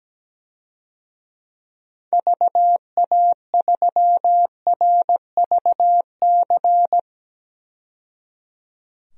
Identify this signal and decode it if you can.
Morse code, heard as a keyed tone, about 17 wpm, 700 Hz. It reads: VA3RVC
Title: DXCC-100 Call Signs - 6 Characters - 17wpm